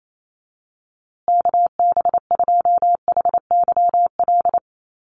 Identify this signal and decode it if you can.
Morse code, heard as a keyed tone, about 28 wpm, 700 Hz. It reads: KB2HYL
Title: USA Call Signs Repeated 2x 28wpm